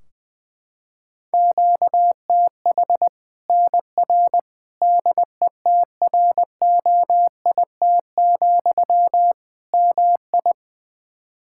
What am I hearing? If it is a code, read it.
Morse code, heard as a keyed tone, about 20 wpm, 700 Hz. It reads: QTH NR DETROIT, MI
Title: K1USN SST QSO Practice 20wpm